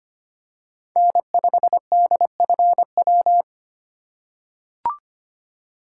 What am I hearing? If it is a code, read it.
Morse code, heard as a keyed tone, about 25 wpm, 700 Hz. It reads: N5DFW
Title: USA Call Signs - 5 Characters - 3x Character Spacing - 25wpm